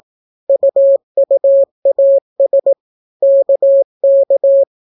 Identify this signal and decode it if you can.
Morse code, heard as a keyed tone, about 18 wpm, 550 Hz. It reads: UUAS KK